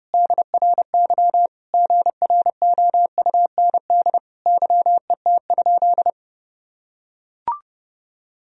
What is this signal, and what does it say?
Morse code, heard as a keyed tone, about 30 wpm, 700 Hz. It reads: DRY GROUND YET?
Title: Sentences from Top 1000 Words - Review Part 3 - 30wpm